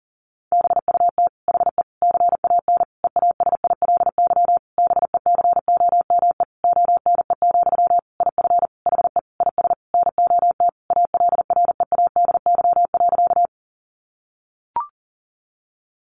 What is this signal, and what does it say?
Morse code, heard as a keyed tone, about 40 wpm, 700 Hz. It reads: BUT HE CAN EASILY BECOME ONE, IF HE IS NOT ALREADY.
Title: Sentences from Top 900 Words 40wpm